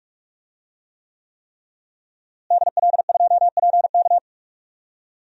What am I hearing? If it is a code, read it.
Morse code, heard as a keyed tone, about 41 wpm, 700 Hz. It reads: DL2PK
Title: DXCC-100 Call Signs - 5 Characters - Repeated 3x - 45wpm